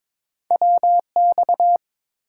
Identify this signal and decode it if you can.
Morse code, heard as a keyed tone, about 23 wpm, 700 Hz. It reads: WX
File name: Top 90 QSO Elements - 8x Character Spacing - 22wpm